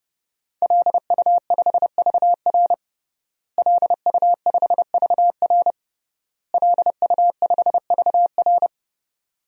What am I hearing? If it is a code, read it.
Morse code, heard as a keyed tone, about 30 wpm, 700 Hz. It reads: LU5VR LU5VR LU5VR
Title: DXCC-100 Call Signs - 5 Characters - Repeated 3x - 30wpm